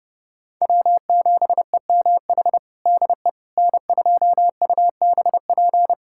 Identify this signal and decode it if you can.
Morse code, heard as a keyed tone, about 32 wpm, 700 Hz. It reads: W7EMH DE N2UBP
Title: USA Callsign DE Callsign - 5 Characters - 30wpm